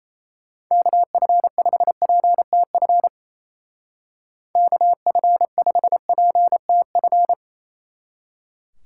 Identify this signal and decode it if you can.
Morse code, heard as a keyed tone, about 34 wpm, 700 Hz. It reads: KF5PTF KF5PTF
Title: Speed Racing - USA 2x3 Call Signs 28wpm